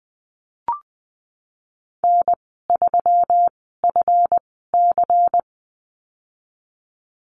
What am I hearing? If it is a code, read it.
Morse code, heard as a keyed tone, about 20 wpm, spaced as for 15 wpm, 700 Hz. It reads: N3FC
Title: 140 Lesson 23g - Letter f - Call Signs 20wpm